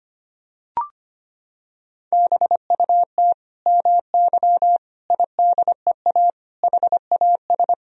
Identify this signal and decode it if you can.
Morse code, heard as a keyed tone, about 25 wpm, 700 Hz. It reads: BUT MY IDEA HAS
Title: Sentences from Top 1000 Words - Review Part 3 - 25wpm